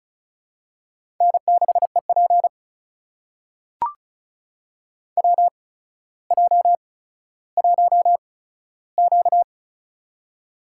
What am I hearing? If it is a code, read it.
Morse code, heard as a keyed tone, about 36 wpm, spaced as for 15 wpm, 700 Hz. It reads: N6EP WJ1Q
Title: USA Call Signs - 8x Character Spacing - 35wpm